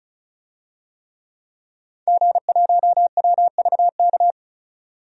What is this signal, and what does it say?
Morse code, heard as a keyed tone, about 36 wpm, 700 Hz. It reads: G1WVK
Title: DXCC-100 Call Signs - 5 Characters - 35wpm